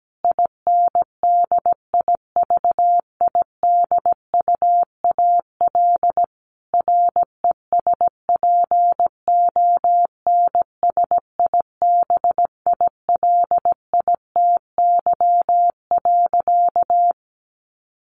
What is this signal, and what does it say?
Morse code, heard as a keyed tone, about 17 wpm, 700 Hz. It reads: INDIVIDUAL RESPONSIBILITY.
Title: Sentences from Top 1500 Words 17wpm